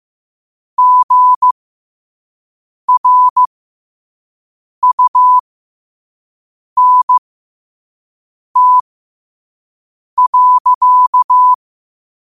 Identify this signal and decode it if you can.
Morse code, heard as a keyed tone, about 15 wpm, spaced as for 5 wpm, 1000 Hz. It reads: GRUNT.
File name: qotdfiles / 5wpm